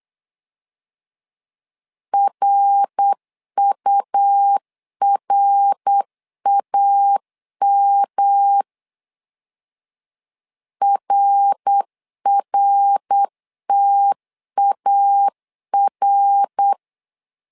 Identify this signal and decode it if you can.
Morse code, heard as a keyed tone, about 8 wpm, 800 Hz. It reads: RURAM RRTAR
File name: LEZIONI / 3 MWRO / 3 MWR 8wpm 2444